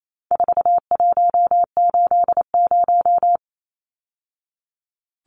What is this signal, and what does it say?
Morse code, heard as a keyed tone, about 28 wpm, 700 Hz. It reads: 4180
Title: Numbers - 4-Digit 28wpm